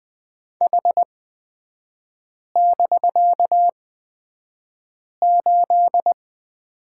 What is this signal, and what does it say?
Morse code, heard as a keyed tone, about 20 wpm, spaced as for 5 wpm, 700 Hz. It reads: H<BK>8
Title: 250 Lesson 40e - BK - Three Characters 20wpm